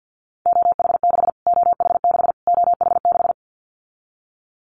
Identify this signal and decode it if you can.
Morse code, heard as a keyed tone, about 50 wpm, 700 Hz. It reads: OH6 OH6 OH6
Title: Callsign Prefixes for Top 25 DXCC - Repeated 3x - 50wpm